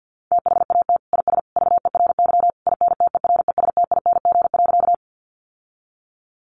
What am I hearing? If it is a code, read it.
Morse code, heard as a keyed tone, about 50 wpm, 700 Hz. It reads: THAT IS VERY INTERESTING.
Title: Sentences from Top 1000 Words 50wpm